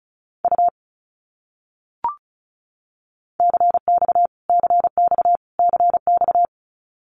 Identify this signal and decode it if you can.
Morse code, heard as a keyed tone, about 34 wpm, 700 Hz. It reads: U CX CX CX
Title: Abbreviations Repeated 3x - 35wpm